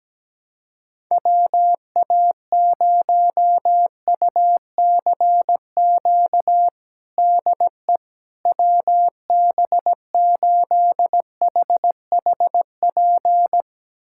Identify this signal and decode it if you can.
Morse code, heard as a keyed tone, about 17 wpm, 700 Hz. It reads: WA0UCQ DE WB8HHP
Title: USA Callsign DE Callsign - 6 Characters - 17wpm